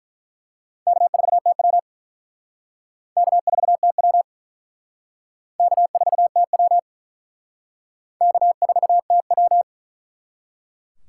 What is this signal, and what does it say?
Morse code, heard as a keyed tone, about 48 wpm, 700 Hz. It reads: K4TW K4TW K4TW K4TW